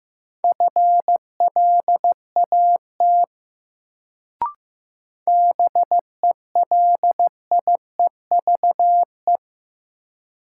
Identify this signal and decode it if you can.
Morse code, heard as a keyed tone, about 15 wpm, 700 Hz. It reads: FLAT BELIEVE
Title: Sentences from Top 1000 Words - New Word Drill - 15wpm